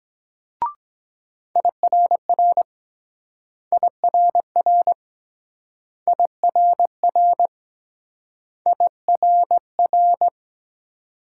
Morse code, heard as a keyed tone, about 23 wpm, 700 Hz. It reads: IRR IRR IRR IRR